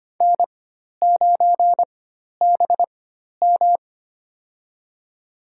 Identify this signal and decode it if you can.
Morse code, heard as a keyed tone, about 25 wpm, spaced as for 12 wpm, 690 Hz. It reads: N9BM
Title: USA Call Signs - 4 Characters - 4x Character Spacing - 25wpm